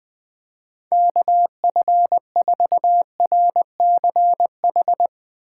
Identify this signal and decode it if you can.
Morse code, heard as a keyed tone, about 20 wpm, 700 Hz. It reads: KF4RCH
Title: Speed Racing - USA 2x3 Call Signs 20wpm